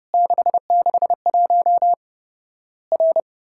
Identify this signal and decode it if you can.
Morse code, heard as a keyed tone, about 30 wpm, 700 Hz. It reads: 661 R
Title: Worked All Europe DX Contest - QTC Exchange - 30wpm